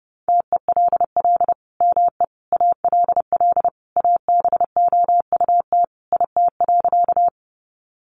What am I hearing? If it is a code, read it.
Morse code, heard as a keyed tone, about 30 wpm, 700 Hz. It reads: TELL ME ALL ABOUT IT.
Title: Sentences from Top 200 Words 30wpm